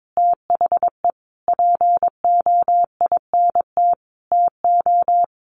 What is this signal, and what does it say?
Morse code, heard as a keyed tone, about 22 wpm, 700 Hz. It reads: THE POINT TO
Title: Sentences from Top 1100 Words 22wpm